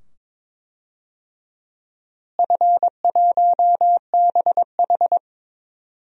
Morse code, heard as a keyed tone, about 22 wpm, 700 Hz. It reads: F1BH
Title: DXCC-100 Call Signs - 4 Characters - Repeated 2x - 22wpm